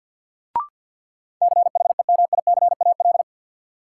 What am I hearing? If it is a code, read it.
Morse code, heard as a keyed tone, about 48 wpm, 700 Hz. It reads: CHEMICAL
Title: Sentences from Top 1200 Words - New Word Drill - 50wpm